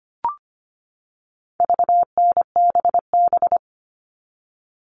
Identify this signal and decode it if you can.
Morse code, heard as a keyed tone, about 25 wpm, 700 Hz. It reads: VNBB
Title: All Cut Numbers - 4-Digit 25wpm